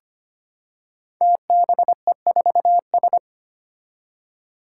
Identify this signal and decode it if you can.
Morse code, heard as a keyed tone, about 25 wpm, 700 Hz. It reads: TBE4S
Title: Five Challenging Letter-Number Combinations 25wpm